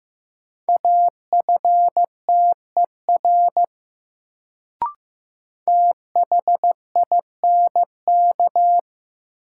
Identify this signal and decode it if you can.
Morse code, heard as a keyed tone, about 15 wpm, 700 Hz. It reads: AFTER THINK
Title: Sentences from Top 200 Words - New Word Drill - 15wpm